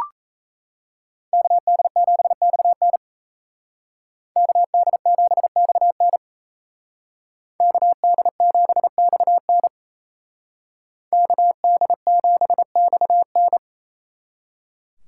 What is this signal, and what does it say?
Morse code, heard as a keyed tone, about 32 wpm, 700 Hz. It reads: KD7XN KD7XN KD7XN KD7XN
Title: Speed Racing - USA 2x2 Call Signs 28wpm